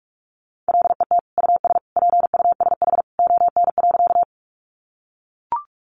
Morse code, heard as a keyed tone, about 44 wpm, 700 Hz. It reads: LET US PUSH ON.
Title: Sentences from Top 1300 Words 45wpm